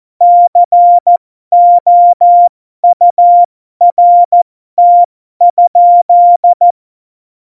Morse code, heard as a keyed tone, about 14 wpm, spaced as for 12 wpm, 700 Hz. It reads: COURT?
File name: Morsekurs / Witze